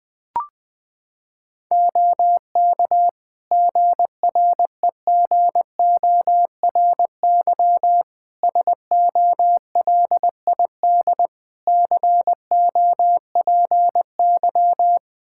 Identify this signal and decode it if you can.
Morse code, heard as a keyed tone, about 21 wpm, 700 Hz. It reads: OK GREGORY SOLID COPY
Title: Practice QSO - Exchange 3 - More Info - 20wpm